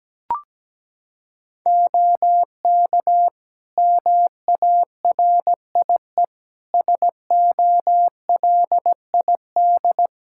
Morse code, heard as a keyed tone, about 17 wpm, 700 Hz. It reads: OK MARIE SOLID
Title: Practice QSO - Exchange 3 - More Info - 17wpm